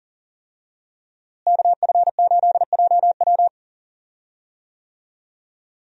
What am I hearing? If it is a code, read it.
Morse code, heard as a keyed tone, about 40 wpm, 700 Hz. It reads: KF8JW
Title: USA Call Signs - 5 Characters - Repeated 2x - 40wpm